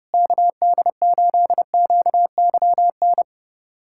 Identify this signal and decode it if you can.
Morse code, heard as a keyed tone, about 30 wpm, 700 Hz. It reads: KD8QYN
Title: USA Callsign DE Callsign - 6 Characters - 30wpm